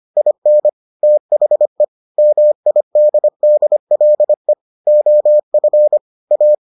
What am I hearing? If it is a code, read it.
Morse code, heard as a keyed tone, about 25 wpm, 600 Hz. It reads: IN THE MIDDLE OF A